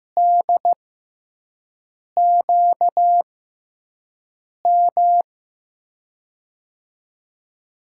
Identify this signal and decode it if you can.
Morse code, heard as a keyed tone, about 15 wpm, spaced as for 5 wpm, 700 Hz. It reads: DQM
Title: USA Call Signs - 6x Character Spacing - 15wpm